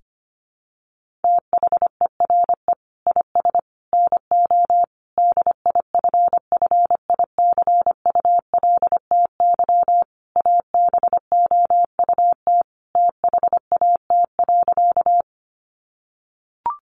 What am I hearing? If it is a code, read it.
Morse code, heard as a keyed tone, about 25 wpm, 700 Hz. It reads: THERE IS NO DIFFICULTY ABOUT THAT.
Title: Sentences from Top 1500 Words 25wpm